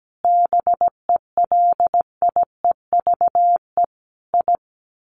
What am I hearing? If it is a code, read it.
Morse code, heard as a keyed tone, about 17 wpm, 700 Hz. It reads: BELIEVE I
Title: Sets of 5 Words Encore2 17wpm